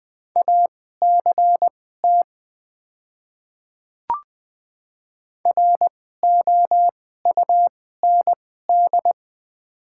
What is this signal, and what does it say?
Morse code, heard as a keyed tone, about 20 wpm, spaced as for 14 wpm, 700 Hz. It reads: ACT ROUND
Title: 2x Character Spacing - Top 500 Words - 20wpm